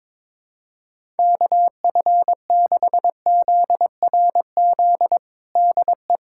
Morse code, heard as a keyed tone, about 22 wpm, 700 Hz. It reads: KF6ZRZ DE
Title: USA Callsign DE Callsign 22wpm